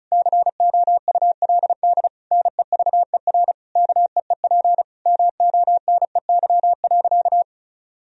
Morse code, heard as a keyed tone, about 34 wpm, 700 Hz. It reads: COULD NEVER KEEP MONEY.
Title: Sentences from Top 1000 Words - Review Part 4 - 35wpm